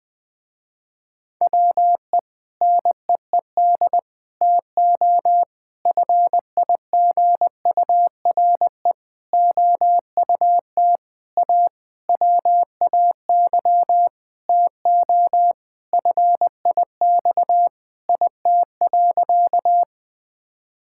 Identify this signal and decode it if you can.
Morse code, heard as a keyed tone, about 20 wpm, 700 Hz. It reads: WE NEED TO FIGURE OUT A WAY TO FIX IT.
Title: Sentences with 6-Word Phrases - Spoken Beginning - 20wpm